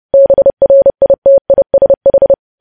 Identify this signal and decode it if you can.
Morse code, heard as a keyed tone, about 30 wpm, 550 Hz. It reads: BRITISH
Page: news headlines in morse code at 30 wpm